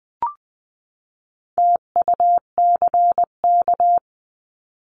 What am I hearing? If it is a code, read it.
Morse code, heard as a keyed tone, about 20 wpm, 700 Hz. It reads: TUCK
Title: 259 Lesson 42d - Words 18-20wpm